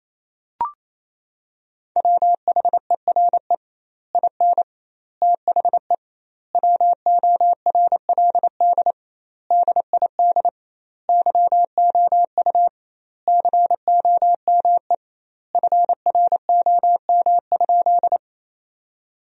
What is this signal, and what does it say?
Morse code, heard as a keyed tone, about 28 wpm, 700 Hz. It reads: WHERE IN THE WORLD DID YOU COME FROM?
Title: 2x Word-Spacing - Sentences from Top 200 Words - 28wpm